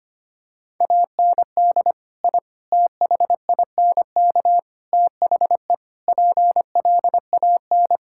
Morse code, heard as a keyed tone, about 25 wpm, 700 Hz. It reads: AND I THINK THE PLAN
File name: practice / 25wpm / Sentences with 4-Word Phrases - 25wpm